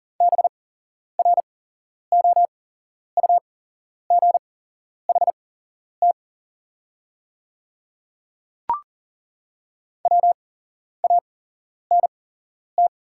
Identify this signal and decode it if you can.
Morse code, heard as a keyed tone, about 40 wpm, spaced as for 11 wpm, 700 Hz. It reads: BROUGHT WANT
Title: 8x Character Spacing - Top 500 Words - 40wpm